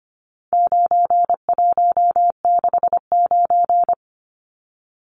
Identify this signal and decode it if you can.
Morse code, heard as a keyed tone, about 25 wpm, 700 Hz. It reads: 9169
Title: Numbers - 4-Digit 25wpm